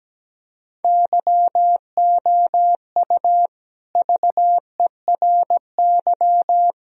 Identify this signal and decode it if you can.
Morse code, heard as a keyed tone, about 17 wpm, 700 Hz. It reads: YOU VERY
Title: Speed-Racing - Sets of 3 Words 17wpm